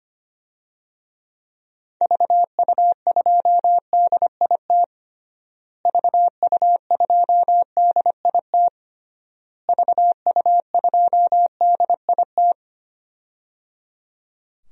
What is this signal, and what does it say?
Morse code, heard as a keyed tone, about 25 wpm, 700 Hz. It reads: VU2DIT VU2DIT VU2DIT
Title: DXCC-100 Call Signs - 6 Characters - Repeated 3x - 25wpm